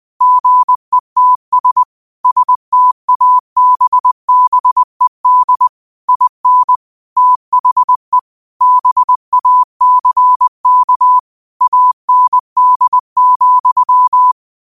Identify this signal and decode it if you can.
Morse code, heard as a keyed tone, about 20 wpm, 1000 Hz. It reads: GETS STABBED IN THE BACK AND,